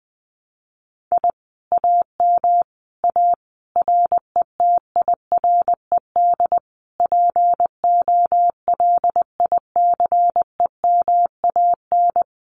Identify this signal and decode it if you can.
Morse code, heard as a keyed tone, about 20 wpm, 700 Hz. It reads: I AM A RETIRED POLICEMAN
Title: Common QSO Phrases Encore 20wpm